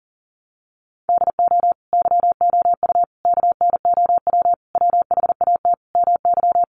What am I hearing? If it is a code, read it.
Morse code, heard as a keyed tone, about 40 wpm, 700 Hz. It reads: DO YOU KNOW WHAT MY
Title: Sentences from Top 1200 Words 40wpm